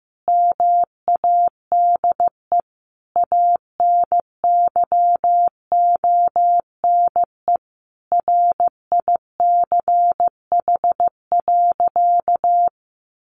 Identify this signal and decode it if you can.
Morse code, heard as a keyed tone, about 15 wpm, 700 Hz. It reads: MADE ANYONE RICH.
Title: Sentences from Top 1100 Words 15wpm